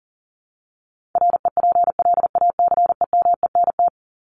Morse code, heard as a keyed tone, about 40 wpm, 700 Hz. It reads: REPLACEMENT
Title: Top 1751-2000 Words 40wpm